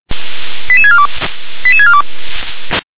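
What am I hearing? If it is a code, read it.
Five-tone selective call (ZVEI), heard as quick runs of five tones, tones of about 70 ms. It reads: 88642 88642
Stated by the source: Five-tone ZVEI selective call 88277, received by SDR.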